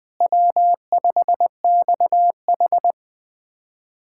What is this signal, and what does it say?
Morse code, heard as a keyed tone, about 20 wpm, 700 Hz. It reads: W5XH
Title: USA Callsign DE Callsign - 4 Characters - 20wpm